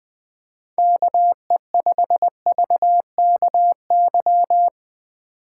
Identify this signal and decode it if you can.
Morse code, heard as a keyed tone, about 21 wpm, 700 Hz. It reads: KE5VKY
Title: Speed Racing - USA 2x3 Call Signs 17wpm